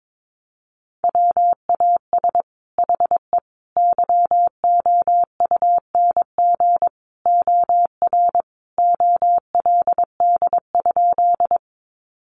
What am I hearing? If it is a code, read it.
Morse code, heard as a keyed tone, about 22 wpm, 700 Hz. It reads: WAS HE YOUNG OR OLD?
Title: Sentences from Top 400 Words 22wpm